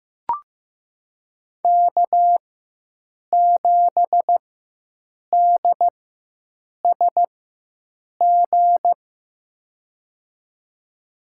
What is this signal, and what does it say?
Morse code, heard as a keyed tone, about 15 wpm, spaced as for 7 wpm, 700 Hz. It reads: K7DSG
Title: USA Call Signs - 5 Characters - 4x Character Spacing - 15wpm